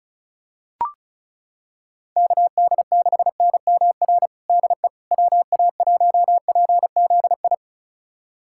Morse code, heard as a keyed tone, about 36 wpm, 700 Hz. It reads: KD6NMR DE WA1PZI
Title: USA Callsign DE Callsign 35wpm